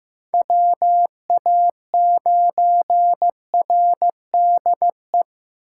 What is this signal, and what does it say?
Morse code, heard as a keyed tone, about 18 wpm, 700 Hz. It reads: WA9RDE
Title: Speed Racing - USA 2x3 Call Signs 15wpm